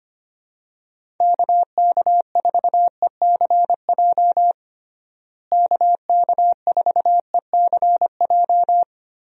Morse code, heard as a keyed tone, about 25 wpm, 700 Hz. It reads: KK4ECJ KK4ECJ
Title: USA Call Signs - 6 Characters - Repeated 3x - 25wpm